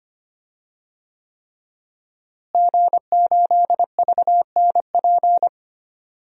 Morse code, heard as a keyed tone, about 25 wpm, 700 Hz. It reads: G8VNP
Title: DXCC-100 Call Signs - 5 Characters - Repeated 3x - 25wpm